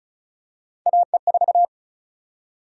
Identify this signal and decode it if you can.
Morse code, heard as a keyed tone, about 36 wpm, 700 Hz. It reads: AE4
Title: USA Call Sign Prefixes - 35wpm